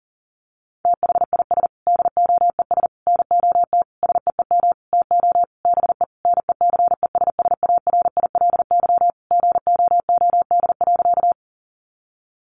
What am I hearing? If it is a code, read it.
Morse code, heard as a keyed tone, about 37 wpm, 700 Hz. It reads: THIS DOES NOT SEEM TO BE NECESSARILY GOOD.
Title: Sentences from Top 1400 Words 40wpm